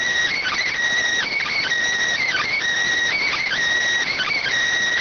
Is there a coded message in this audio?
Yes: Emergency Alert System (SAME) tones from 0.3 to 4.5 s, heard as harsh buzzing bursts.